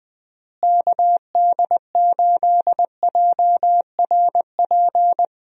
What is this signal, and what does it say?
Morse code, heard as a keyed tone, about 20 wpm, 700 Hz. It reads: KD8JRP
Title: USA Call Signs - 6 Characters - Repeated 3x - 20wpm